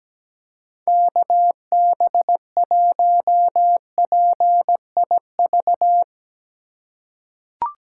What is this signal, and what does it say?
Morse code, heard as a keyed tone, about 17 wpm, 700 Hz. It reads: KB1PIV
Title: USA Call Signs - 6 Characters - 3x Character Spacing - 17wpm